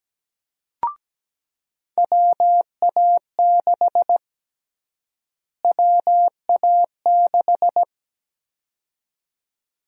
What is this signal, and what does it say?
Morse code, heard as a keyed tone, about 17 wpm, 700 Hz. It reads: WA6 WA6
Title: USA Call Sign Prefixes - Repeated 2x - 17wpm